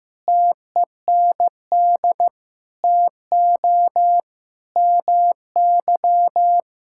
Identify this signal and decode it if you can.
Morse code, heard as a keyed tone, about 16 wpm, 700 Hz. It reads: TEND TO MY
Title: Sentences from Top 1000 Words - Review Part 5 - 15wpm